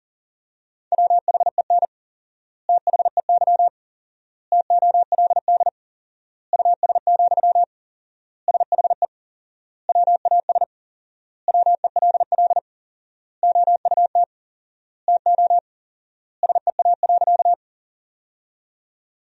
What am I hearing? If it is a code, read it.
Morse code, heard as a keyed tone, about 40 wpm, 700 Hz. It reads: WHEN THEY TOLD US, SHE WAS WELL OUT TO SEA.
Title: 4x Word-Spacing - Sentences from Top 400 Words - 40wpm